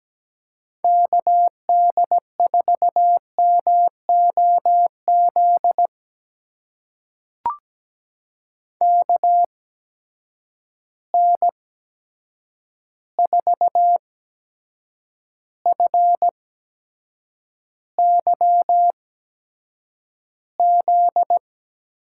Morse code, heard as a keyed tone, about 17 wpm, spaced as for 7 wpm, 700 Hz. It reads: KD4MOZ KN4FYZ
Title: USA Call Signs - 6 Characters - 8x Character Spacing - 17wpm